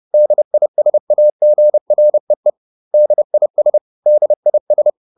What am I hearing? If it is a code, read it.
Morse code, heard as a keyed tone, about 30 wpm, 600 Hz. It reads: DISAGREE DIS DIS